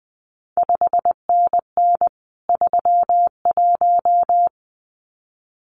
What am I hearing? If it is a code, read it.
Morse code, heard as a keyed tone, about 20 wpm, 700 Hz. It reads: 5NN 31
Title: CQ WW DX Contest Exchange - 20wpm